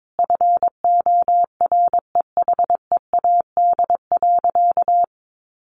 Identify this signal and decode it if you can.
Morse code, heard as a keyed tone, about 22 wpm, 700 Hz. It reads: FOREHEAD.
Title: Easy Phrases 22wpm